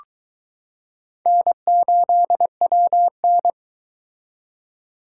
Morse code, heard as a keyed tone, about 22 wpm, 700 Hz. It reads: N8WN